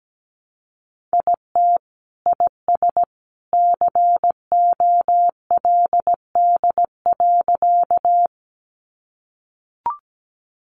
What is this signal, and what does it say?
Morse code, heard as a keyed tone, about 17 wpm, 700 Hz. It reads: IT IS COLD.
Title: Sentences from Top 400 Words 17wpm